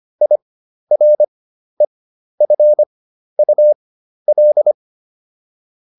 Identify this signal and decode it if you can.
Morse code, heard as a keyed tone, about 25 wpm, spaced as for 12 wpm, 600 Hz. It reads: IREFUL